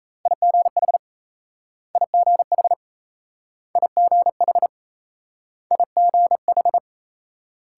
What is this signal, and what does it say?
Morse code, heard as a keyed tone, about 38 wpm, 700 Hz. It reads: IGH IGH IGH IGH